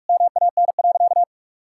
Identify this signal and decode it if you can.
Morse code, heard as a keyed tone, about 46 wpm, 700 Hz. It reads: MAN.
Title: Sentences from Top 1000 Words - Review Part 5 - 45wpm